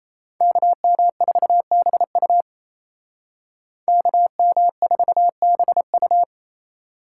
Morse code, heard as a keyed tone, about 34 wpm, 700 Hz. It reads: KM4BU KM4BU
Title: Speed Racing - USA 2x2 Call Signs 28wpm